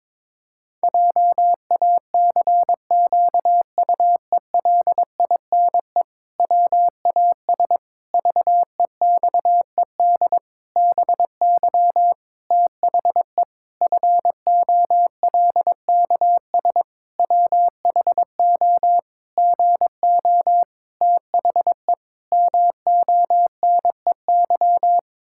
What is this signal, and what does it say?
Morse code, heard as a keyed tone, about 21 wpm, 700 Hz. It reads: JACQUELINE WAS VEXED BY THE FOLKS WHO GO THE MONEY
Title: Pangram Challenge v2 - 22wpm